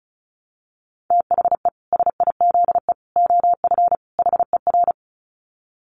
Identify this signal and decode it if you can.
Morse code, heard as a keyed tone, about 34 wpm, 700 Hz. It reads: THE SIZE OF HER
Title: Dit Challenge 35wpm